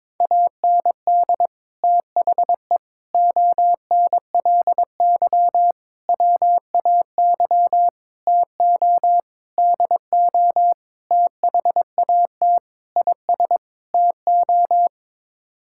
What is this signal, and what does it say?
Morse code, heard as a keyed tone, about 22 wpm, 700 Hz. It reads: AND THE ONLY WAY TO DO THAT IS TO
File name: Sets of 9 Words - 22wpm